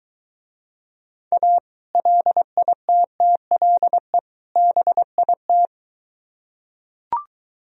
Morse code, heard as a keyed tone, about 23 wpm, 700 Hz. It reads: A LITTLE BIT